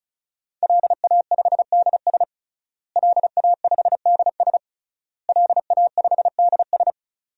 Morse code, heard as a keyed tone, about 36 wpm, 700 Hz. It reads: LA5DS LA5DS LA5DS
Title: DXCC-100 Call Signs - 5 Characters - Repeated 3x - 35wpm